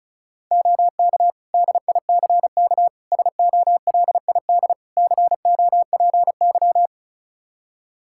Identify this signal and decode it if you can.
Morse code, heard as a keyed tone, about 34 wpm, 700 Hz. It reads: OK DICK SOLID COPY
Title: Practice QSO - Exchange 3 - More Info - 35wpm